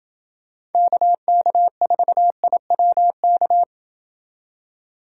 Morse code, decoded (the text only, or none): KK4IWK